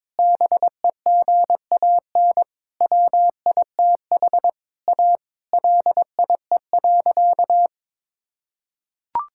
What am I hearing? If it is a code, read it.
Morse code, heard as a keyed tone, about 21 wpm, 700 Hz. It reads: BEGAN WITH A LIE.
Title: Sentences from Top 1600 Words 22wpm